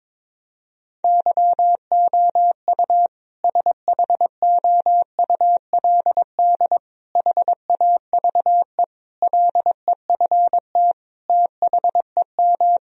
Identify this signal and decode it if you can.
Morse code, heard as a keyed tone, about 22 wpm, 700 Hz. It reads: YOU SHOULD HAVE LEFT THEM
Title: Sentences from Top 1000 Words - Review Part 1 - 22wpm